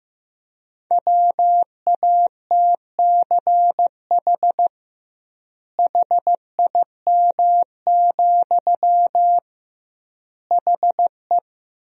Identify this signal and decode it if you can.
Morse code, heard as a keyed tone, about 15 wpm, 700 Hz. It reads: WATCH HIM, HE
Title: 2x Word-Spacing - Sentences from Top 300 Words - 15wpm